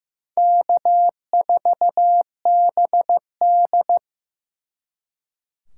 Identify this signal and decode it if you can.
Morse code, heard as a keyed tone, about 18 wpm, 700 Hz. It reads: K4BD
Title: Speed Racing - USA 1x2 Call Signs 15wpm